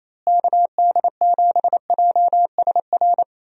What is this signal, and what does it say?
Morse code, heard as a keyed tone, about 28 wpm, 700 Hz. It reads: KD7JSR
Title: USA Call Signs - 6 Characters - Repeated 3x - 28wpm